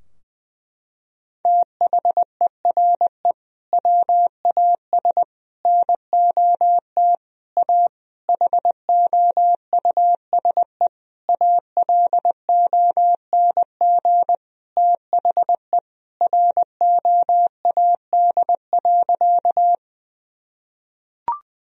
Morse code, heard as a keyed tone, about 21 wpm, 700 Hz. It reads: THERE WAS NOT A HOUSE ALONG THE ROAD.
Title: Sentences from Top 1000 Words - Review Part 2 - 20wpm